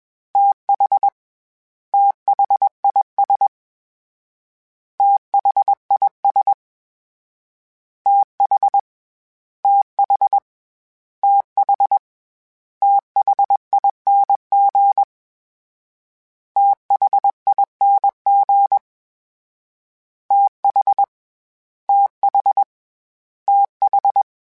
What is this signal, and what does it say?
Morse code, heard as a keyed tone, about 21 wpm, 800 Hz. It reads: TH THIS THIS TH TH TH THING THING TH TH TH
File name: TH / TH mp3s / TH Prefixes 3x2 @20